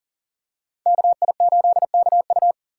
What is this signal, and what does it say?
Morse code, heard as a keyed tone, about 40 wpm, 700 Hz. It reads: KI8KU
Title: USA Call Signs - 2x Character Spacing - 40wpm